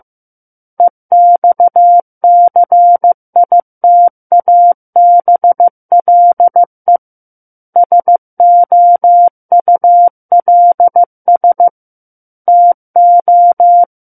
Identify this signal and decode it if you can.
Morse code, heard as a keyed tone, about 15 wpm, 700 Hz. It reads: EXCITABLE SOULS TO
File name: Chapter 6 - 15 WPM